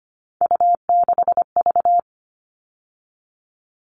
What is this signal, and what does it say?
Morse code, heard as a keyed tone, about 25 wpm, 700 Hz. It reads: U6V
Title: All Cut Numbers - 3-Digit 25wpm